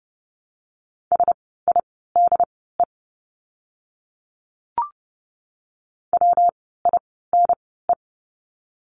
Morse code, heard as a keyed tone, about 30 wpm, spaced as for 17 wpm, 700 Hz. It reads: SIDE WINE
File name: course / 30:10wpm / 115 Lesson 19f - Letter w - Words 30wpm